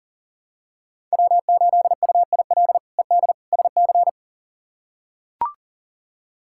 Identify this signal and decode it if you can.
Morse code, heard as a keyed tone, about 37 wpm, 700 Hz. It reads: W8UIL ED SC
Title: K1USN SST QSO Practice - 40wpm